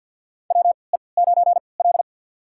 Morse code, heard as a keyed tone, about 46 wpm, spaced as for 27 wpm, 690 Hz. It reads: WE9L
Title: USA Call Signs - 3x Character Spacing - 50wpm